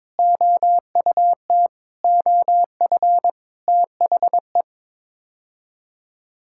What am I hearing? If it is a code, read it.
Morse code, heard as a keyed tone, about 22 wpm, 700 Hz. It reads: OUT OF THE
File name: practice / 22wpm / Sets of 7 Words - 22wpm